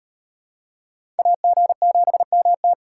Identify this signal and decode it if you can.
Morse code, heard as a keyed tone, about 38 wpm, 700 Hz. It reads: AG7MT